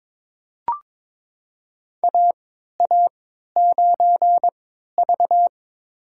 Morse code, heard as a keyed tone, about 22 wpm, spaced as for 13 wpm, 700 Hz. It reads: AA9V
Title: USA Call Signs - 4 Characters - 3x Character Spacing - 22wpm